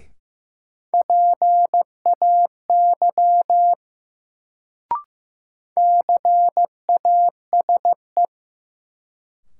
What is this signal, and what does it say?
Morse code, heard as a keyed tone, about 15 wpm, 700 Hz. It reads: PAY CASE
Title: Sentences from Top 700 Words - New Word Drill - 15wpm